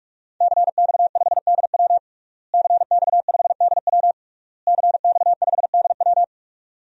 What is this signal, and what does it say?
Morse code, heard as a keyed tone, about 44 wpm, 700 Hz. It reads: CX5DW CX5DW CX5DW
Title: DXCC-100 Call Signs - 5 Characters - Repeated 3x - 45wpm